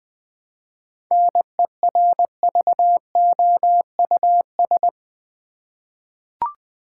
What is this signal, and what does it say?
Morse code, heard as a keyed tone, about 20 wpm, 700 Hz. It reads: NERVOUS
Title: Seven Letter Words - 20wpm